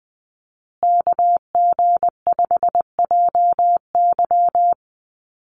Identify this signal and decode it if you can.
Morse code, heard as a keyed tone, about 20 wpm, 700 Hz. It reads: KG5JY
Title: USA Call Signs Repeated 3x 20wpm